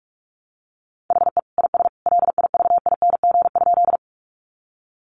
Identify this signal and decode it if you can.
Morse code, heard as a keyed tone, about 44 wpm, 700 Hz. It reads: HE IS LIVING?
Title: Sentences from Top 1100 Words 45wpm